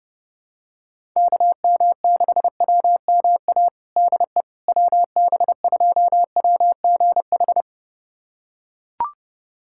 Morse code, heard as a keyed tone, about 30 wpm, 700 Hz. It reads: KM6WMA DE WB2WGH
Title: USA Callsign DE Callsign - 6 Characters - 30wpm